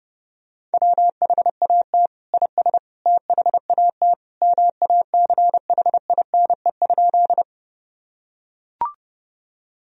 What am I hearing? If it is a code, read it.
Morse code, heard as a keyed tone, about 30 wpm, 700 Hz. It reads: WHAT IS THAT MACHINE?
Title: Sentences from Top 1000 Words - Review Part 2 - 30wpm